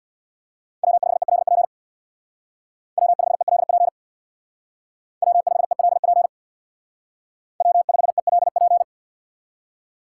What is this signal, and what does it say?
Morse code, heard as a keyed tone, about 59 wpm, 700 Hz. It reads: W5ELP W5ELP W5ELP W5ELP
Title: Speed Racing - USA 1x3 Call Signs 50wpm